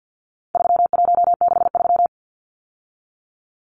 Morse code, heard as a keyed tone, about 50 wpm, 700 Hz. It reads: 3163
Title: Numbers - 4-Digit 50wpm